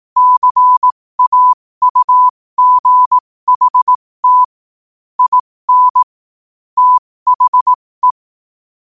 Morse code, heard as a keyed tone, about 18 wpm, spaced as for 15 wpm, 1000 Hz. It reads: CAUGHT IN THE
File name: qotdfiles / 15wpm